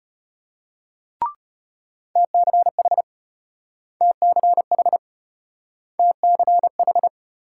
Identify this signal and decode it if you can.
Morse code, heard as a keyed tone, about 34 wpm, 700 Hz. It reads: TCH TCH TCH